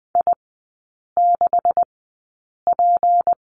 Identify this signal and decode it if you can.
Morse code, heard as a keyed tone, about 20 wpm, spaced as for 8 wpm, 700 Hz. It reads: I6P
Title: 172 Lesson 28e - Number 9 - Three Characters 20wpm